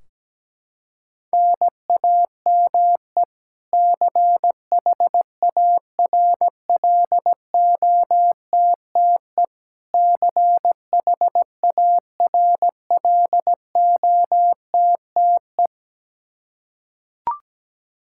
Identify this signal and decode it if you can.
Morse code, heard as a keyed tone, about 17 wpm, 700 Hz. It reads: NAME CHARLOTTE CHARLOTTE
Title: Practice QSO - Exchange 1 - Beginning - 17wpm